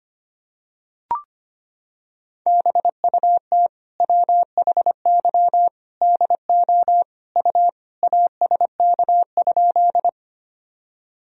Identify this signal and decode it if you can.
Morse code, heard as a keyed tone, about 25 wpm, 700 Hz. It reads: BUT WHY DO U ASK?
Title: Sentences from Top 200 Words and Abbreviations 25wpm